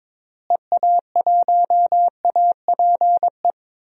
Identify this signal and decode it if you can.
Morse code, heard as a keyed tone, about 21 wpm, 700 Hz. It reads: EA1APE
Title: DXCC-100 Call Signs - 6 Characters - Repeated 2x - 22wpm